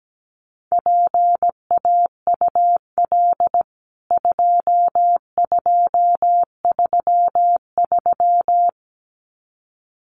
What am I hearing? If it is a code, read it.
Morse code, heard as a keyed tone, about 17 wpm, 700 Hz. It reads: PAUL 2233
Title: CWT Exchange 17wpm